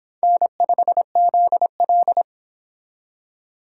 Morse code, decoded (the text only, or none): N5ZL